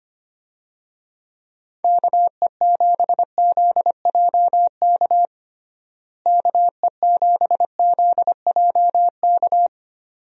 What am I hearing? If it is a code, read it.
Morse code, heard as a keyed tone, about 25 wpm, 700 Hz. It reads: KE7ZJK KE7ZJK
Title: USA Call Signs - 6 Characters - Repeated 2x - 25wpm